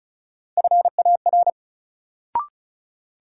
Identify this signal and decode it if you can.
Morse code, heard as a keyed tone, about 36 wpm, 700 Hz. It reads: FAR
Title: Sets of 3 Words - New Word Drill - 35wpm